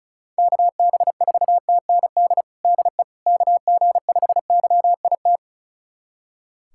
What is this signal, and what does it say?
Morse code, heard as a keyed tone, about 36 wpm, 700 Hz. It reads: KB4TND DE KG5YIT
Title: USA Callsign DE Callsign - 6 Characters - 35wpm